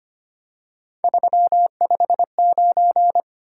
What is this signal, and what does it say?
Morse code, heard as a keyed tone, about 25 wpm, 700 Hz. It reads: 359
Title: Top 180 QSO Elements 25wpm